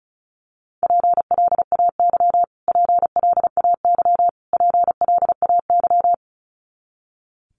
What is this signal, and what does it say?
Morse code, heard as a keyed tone, about 34 wpm, 700 Hz. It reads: PLAY PLAY PLAY
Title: Top 200 Words Repeated 35wpm